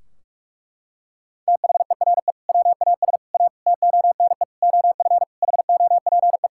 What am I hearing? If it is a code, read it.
Morse code, heard as a keyed tone, about 46 wpm, 700 Hz. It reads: THERE WAS A TONE OF HOPE
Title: Sentences from Top 1600 Words 45wpm